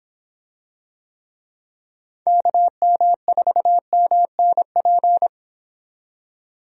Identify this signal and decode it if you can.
Morse code, heard as a keyed tone, about 25 wpm, 700 Hz. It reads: KM4MNP